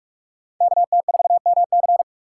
Morse code, heard as a keyed tone, about 47 wpm, 700 Hz. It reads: KT4MC
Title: USA Call Signs - 8x Character Spacing - 45wpm